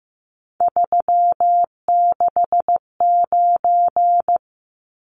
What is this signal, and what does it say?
Morse code, heard as a keyed tone, about 15 wpm, 700 Hz. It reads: 369
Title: Numbers - 3-Digit 15wpm